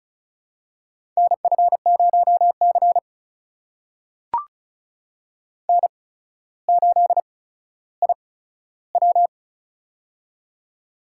Morse code, heard as a keyed tone, about 34 wpm, spaced as for 15 wpm, 700 Hz. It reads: NF0C N8IW
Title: USA Call Signs - 4 Characters - 8x Character Spacing - 35wpm